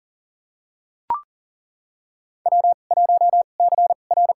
Morse code, heard as a keyed tone, about 40 wpm, spaced as for 29 wpm, 700 Hz. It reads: W1CR
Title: USA Call Signs - 2x Character Spacing - 40wpm